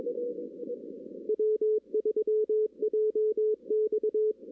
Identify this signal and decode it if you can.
Morse code, heard as a keyed tone, about 22 wpm, 410 Hz. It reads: W3JX
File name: 2016 arrl dx cw / 2016-ARRL-DX-CW-K5TR- - 0-22